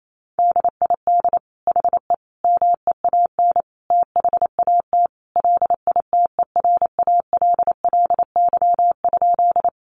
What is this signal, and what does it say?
Morse code, heard as a keyed tone, about 28 wpm, 700 Hz. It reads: DID HE MEAN THAT LITERALLY?
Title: Sentences from Top 1400 Words 28wpm